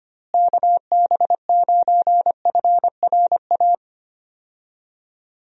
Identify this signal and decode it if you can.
Morse code, heard as a keyed tone, about 25 wpm, 700 Hz. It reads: KB9FRA
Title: USA Call Signs - 6 Characters - Repeated 3x - 25wpm